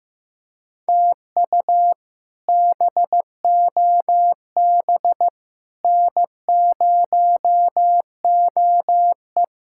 Morse code, heard as a keyed tone, about 15 wpm, 700 Hz. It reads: TU BOB N0OE
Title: K1USN SST QSO Practice - 15wpm